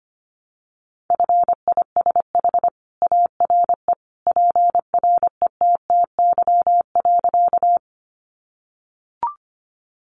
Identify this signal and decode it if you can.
Morse code, heard as a keyed tone, about 25 wpm, 700 Hz. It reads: FISH ARE PRETTY.
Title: Additional Easy Phrases 25wpm